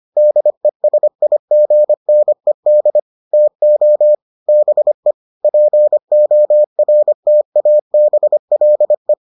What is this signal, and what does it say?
Morse code, heard as a keyed tone, about 25 wpm, 600 Hz. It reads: DESIGNED TO BE PORTABLE